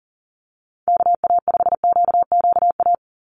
Morse code, heard as a keyed tone, about 40 wpm, 700 Hz. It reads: KA5QQA
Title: NA SOTA Chaser Callsigns 40wpm